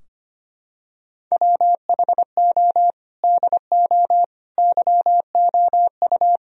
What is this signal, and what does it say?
Morse code, heard as a keyed tone, about 25 wpm, 700 Hz. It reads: WHO DO YOU
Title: Sentences from Top 1000 Words - Review Part 4 - 25wpm